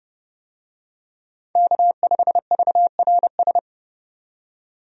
Morse code, heard as a keyed tone, about 30 wpm, 700 Hz. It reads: K5VRS
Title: USA Call Signs - 5 Characters - 30wpm